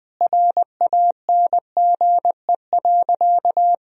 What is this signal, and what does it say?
Morse code, heard as a keyed tone, about 20 wpm, 700 Hz. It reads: RANGE.
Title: Sentences from Top 1000 Words - Review Part 4 - 20wpm